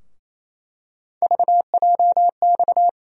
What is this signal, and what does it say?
Morse code, heard as a keyed tone, about 28 wpm, 700 Hz. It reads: VJX
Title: USA Call Sign Postfixes - 3 Letters 28wpm